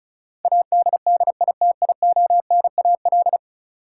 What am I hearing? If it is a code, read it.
Morse code, heard as a keyed tone, about 36 wpm, 700 Hz. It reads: ADDITIONAL
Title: Sentences from Top 1000 Words - Review Part 4 - New Word Drill - 35wpm